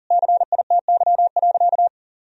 Morse code, heard as a keyed tone, about 37 wpm, 700 Hz. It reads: CITY.
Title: Sentences from Top 1000 Words - Review Part 2 - 40wpm